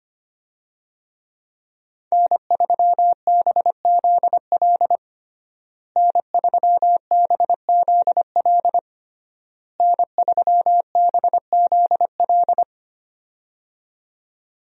Morse code, heard as a keyed tone, about 25 wpm, 700 Hz. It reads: N3BZL N3BZL N3BZL
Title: USA Call Signs - 5 Characters - Repeated 3x - 25wpm